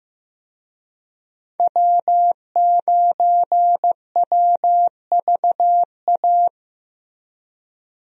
Morse code, heard as a keyed tone, about 15 wpm, 700 Hz. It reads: W9WVA
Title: USA Call Signs - 5 Characters - Repeated 2x - 15wpm